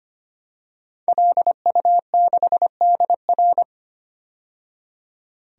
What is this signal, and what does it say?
Morse code, heard as a keyed tone, about 25 wpm, 700 Hz. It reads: LU6DR
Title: DXCC-100 Call Signs - 5 Characters - 25wpm